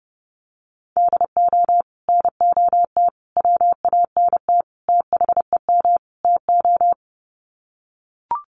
Morse code, heard as a keyed tone, about 30 wpm, 700 Hz. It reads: DO NOT WANT THEM TO
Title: Sets of 5 Words Encore 30wpm